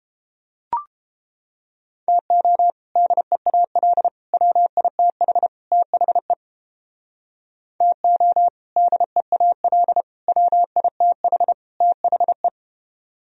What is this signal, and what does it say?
Morse code, heard as a keyed tone, about 29 wpm, 700 Hz. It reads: TO DEAL WITH THE TO DEAL WITH THE